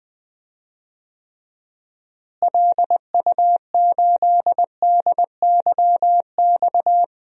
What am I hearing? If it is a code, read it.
Morse code, heard as a keyed tone, about 20 wpm, 700 Hz. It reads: LU8DYX
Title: DXCC-100 Call Signs - 6 Characters - Repeated 3x - 20wpm